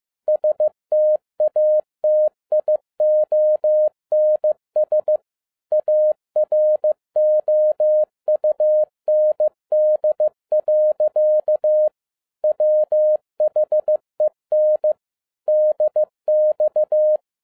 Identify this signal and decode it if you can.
Morse code, heard as a keyed tone, about 15 wpm, 610 Hz. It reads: STATIONS AROUND. WHEN DX